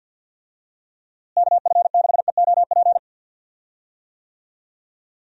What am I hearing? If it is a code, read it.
Morse code, heard as a keyed tone, about 51 wpm, 700 Hz. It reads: KF6EOP
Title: USA Call Signs - 6 Characters - Repeated 3x - 50wpm